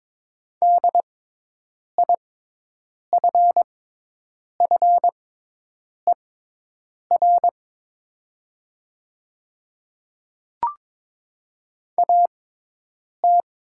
Morse code, heard as a keyed tone, about 22 wpm, spaced as for 8 wpm, 700 Hz. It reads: DIFFER AT
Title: 6x Character Spacing - Top 200 Words - 22wpm